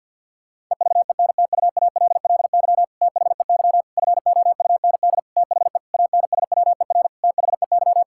Morse code, heard as a keyed tone, about 50 wpm, 700 Hz. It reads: EVENTUALLY THEY FOUND THE ANSWER THEY
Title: Sentences from Top 1000 Words - Review Part 6 - 50wpm